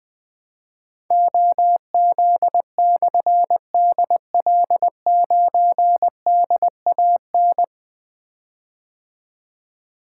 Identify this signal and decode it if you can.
Morse code, heard as a keyed tone, about 20 wpm, 700 Hz. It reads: OZ/DL9DAN
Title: Difficult Call Signs 20wpm